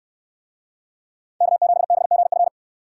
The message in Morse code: K7BGF

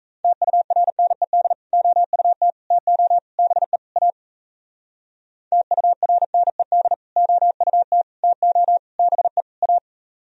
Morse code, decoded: TURNED OUT TO BE A TURNED OUT TO BE A

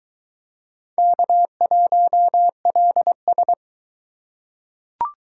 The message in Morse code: K1LS

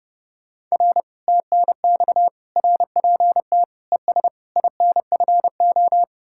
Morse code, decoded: R TNX RPT ES INFO

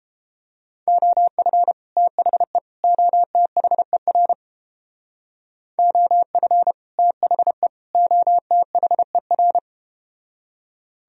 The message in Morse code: OF THE OTHER OF THE OTHER

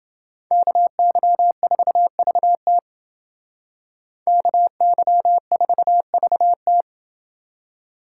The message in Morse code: KY4VT KY4VT

